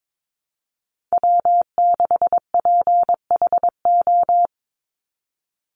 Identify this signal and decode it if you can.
Morse code, heard as a keyed tone, about 22 wpm, 700 Hz. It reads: W6PHO
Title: Call Signs - 22wpm